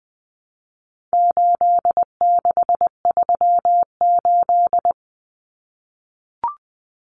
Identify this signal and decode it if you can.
Morse code, heard as a keyed tone, about 20 wpm, 700 Hz. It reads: 8638
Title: Common Cut Numbers - 4-Digit 20wpm